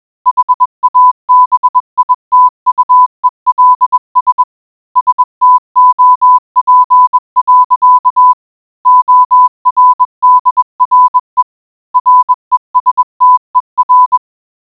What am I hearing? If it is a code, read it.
Morse code, heard as a keyed tone, about 21 wpm, 1000 Hz. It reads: HABITUELS STOP. ORDRE RESTER